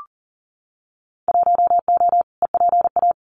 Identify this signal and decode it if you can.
Morse code, heard as a keyed tone, about 40 wpm, 700 Hz. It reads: 1O EPA
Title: Winter Field Day Exchanges 40wpm